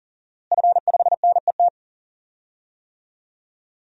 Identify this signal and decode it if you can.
Morse code, heard as a keyed tone, about 40 wpm, 700 Hz. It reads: F5NET
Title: DXCC-100 Call Signs - 5 Characters - Repeated 3x - 40wpm